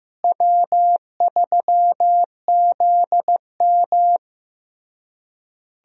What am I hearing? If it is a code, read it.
Morse code, heard as a keyed tone, about 15 wpm, 700 Hz. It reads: W3ZM
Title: USA Call Signs - 4 Characters - Repeated 3x - 15wpm